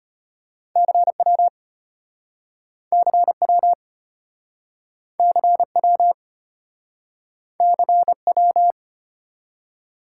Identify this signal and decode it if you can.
Morse code, heard as a keyed tone, about 33 wpm, 700 Hz. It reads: CW CW CW CW